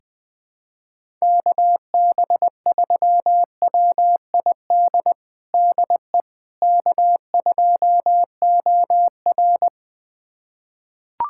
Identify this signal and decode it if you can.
Morse code, heard as a keyed tone, about 20 wpm, 700 Hz. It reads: KB3WID DE K2OR
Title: USA Callsign DE Callsign 20wpm